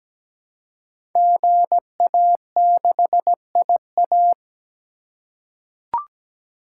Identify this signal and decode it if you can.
Morse code, heard as a keyed tone, about 17 wpm, 700 Hz. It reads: GA6IA
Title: Code Groups - 5 Letter:Numbers - 17wpm